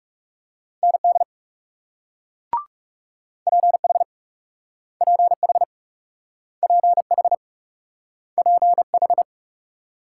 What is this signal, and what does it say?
Morse code, heard as a keyed tone, about 38 wpm, 700 Hz. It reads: ND PH PH PH PH